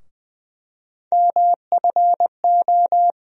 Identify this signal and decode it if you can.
Morse code, heard as a keyed tone, about 20 wpm, 700 Hz. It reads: MFO
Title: USA Call Sign Postfixes - 3 Letters 20wpm